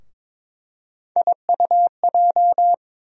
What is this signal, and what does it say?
Morse code, heard as a keyed tone, about 21 wpm, 700 Hz. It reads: IUJ
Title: USA Call Sign Postfixes - 3 Letters 22wpm